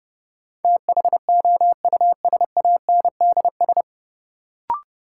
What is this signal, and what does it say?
Morse code, heard as a keyed tone, about 30 wpm, 700 Hz. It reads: THOUSANDS
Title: Sets of 4 Words - New Word Drill - 30wpm